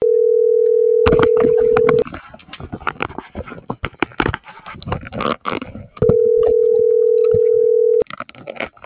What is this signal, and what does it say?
Ringback tone (North America), 440 + 480 Hz.